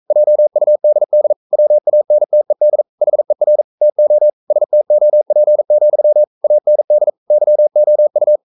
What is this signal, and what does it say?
Morse code, heard as a keyed tone, about 42 wpm, 600 Hz. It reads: JUDD WANTED HER TO STOP, AND YOU
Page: Romanen Never af Ken Follet på engelsk.